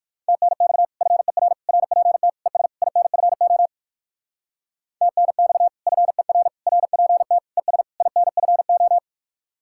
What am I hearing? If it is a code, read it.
Morse code, heard as a keyed tone, about 58 wpm, 700 Hz. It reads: TNX FER RPT ES INFO TNX FER RPT ES INFO